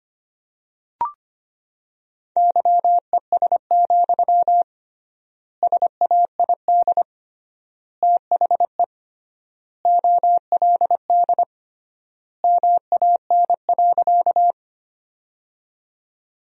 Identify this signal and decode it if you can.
Morse code, heard as a keyed tone, about 25 wpm, 700 Hz. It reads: YES, SAID THE OLD MAN.
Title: 3x Word-Spacing - Sentences from Top 500 Words - 25wpm